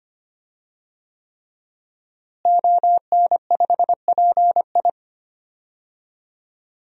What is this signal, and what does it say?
Morse code, heard as a keyed tone, about 25 wpm, 700 Hz. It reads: ON5PI
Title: DXCC-100 Call Signs - 5 Characters - Repeated 3x - 25wpm